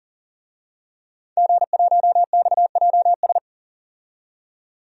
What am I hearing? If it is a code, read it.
Morse code, heard as a keyed tone, about 40 wpm, 700 Hz. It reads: G1XJS
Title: DXCC-100 Call Signs - 5 Characters - 40wpm